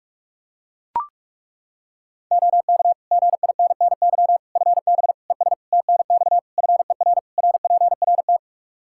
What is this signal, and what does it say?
Morse code, heard as a keyed tone, about 42 wpm, 700 Hz. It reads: OK GINNY FB ES TNX FER RPRT
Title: Practice QSO - Exchange 2 - Info - 45wpm